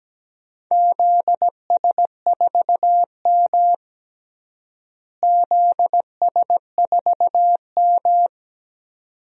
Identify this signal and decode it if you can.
Morse code, heard as a keyed tone, about 17 wpm, 700 Hz. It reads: ZS4M ZS4M
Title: DXCC-100 Call Signs - 4 Characters - Repeated 2x - 17wpm